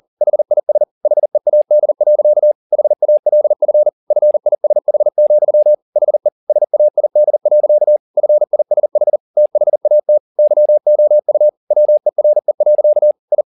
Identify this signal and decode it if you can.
Morse code, heard as a keyed tone, about 40 wpm, 600 Hz. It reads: HIS HEAD. HALF FISH, HE SAID. FISH THAT YOU WERE. I
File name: the old man and the sea 2 40wpm